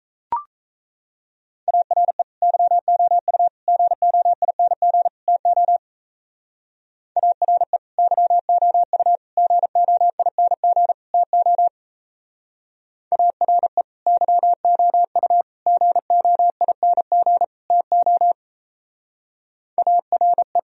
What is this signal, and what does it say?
Morse code, heard as a keyed tone, about 35 wpm, 700 Hz. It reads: ARE YOU GOING TO ARE YOU GOING TO ARE YOU GOING TO ARE
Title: Speed-Racing - Sets of 4 Words 28wpm